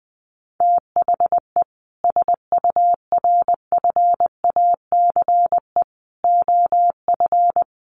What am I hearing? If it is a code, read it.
Morse code, heard as a keyed tone, about 20 wpm, 700 Hz. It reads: THE SURFACE OF
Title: Sets of 4 Words 20wpm